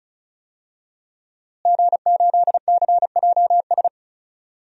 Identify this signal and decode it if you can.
Morse code, heard as a keyed tone, about 34 wpm, 700 Hz. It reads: G8CJS
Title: DXCC-100 Call Signs - 5 Characters - Repeated 3x - 35wpm